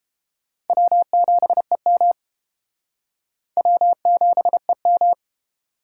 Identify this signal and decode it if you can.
Morse code, heard as a keyed tone, about 28 wpm, 700 Hz. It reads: W7EM W7EM